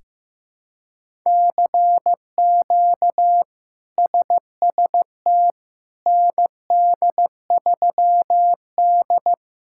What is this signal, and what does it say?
Morse code, heard as a keyed tone, about 15 wpm, 700 Hz. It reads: CQ SST ND3D
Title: K1USN SST QSO Practice - 15wpm